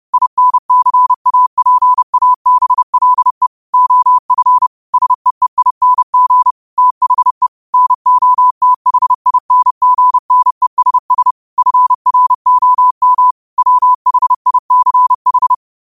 Morse code, heard as a keyed tone, about 30 wpm, 1000 Hz. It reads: INCAPABLE OF SEEING THE NOTHINGNESS FROM WHICH
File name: qotdfiles / 30wpm